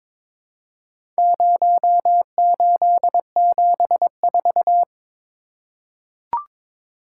Morse code, heard as a keyed tone, about 22 wpm, 700 Hz. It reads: 0874
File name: practice / 22wpm / Four Letter-Number Combinations - 22wpm